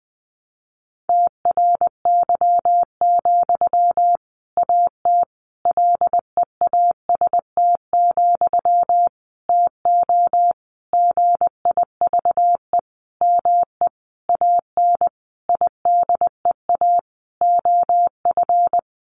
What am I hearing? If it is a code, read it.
Morse code, heard as a keyed tone, about 20 wpm, 700 Hz. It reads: TRY, AT LEAST, TO GIVE ME AN IDEA OF
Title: Sentences from Top 700 Words 20wpm